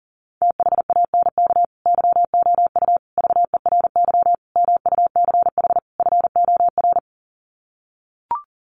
Morse code, heard as a keyed tone, about 40 wpm, 700 Hz. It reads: THANK YOU VERY MUCH FOR
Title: Sets of 5 Words 40wpm